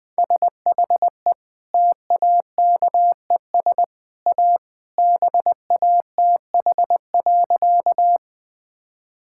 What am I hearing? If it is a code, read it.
Morse code, heard as a keyed tone, about 20 wpm, 700 Hz. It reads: SHE TAKES A BATH.
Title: Easy Sentences with Hints - 20wpm